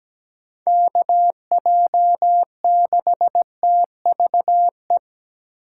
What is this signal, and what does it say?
Morse code, heard as a keyed tone, about 17 wpm, 700 Hz. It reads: KJ6TVE
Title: USA Call Signs - 2x Character Spacing - 17wpm